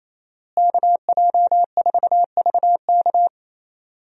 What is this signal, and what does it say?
Morse code, heard as a keyed tone, about 28 wpm, 700 Hz. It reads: KJ4VK
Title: USA Call Signs - 5 Characters - Repeated 3x - 28wpm